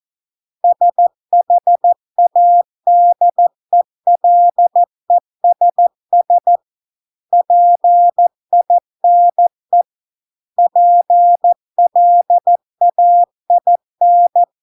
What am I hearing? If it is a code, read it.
Morse code, heard as a keyed tone, about 14 wpm, 700 Hz. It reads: SHADELESS PINE PLAIN